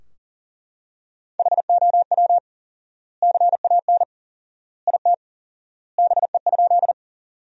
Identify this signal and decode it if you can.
Morse code, heard as a keyed tone, about 40 wpm, 700 Hz. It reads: HOW CAN IT BE?
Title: 4x Word-Spacing - Sentences from Top 100 Words - 40wpm